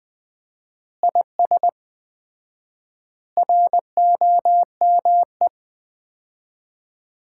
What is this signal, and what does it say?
Morse code, heard as a keyed tone, about 20 wpm, 700 Hz. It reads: IS ROME
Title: Common QSO Phrases - 4x Word-Spacing - 20wpm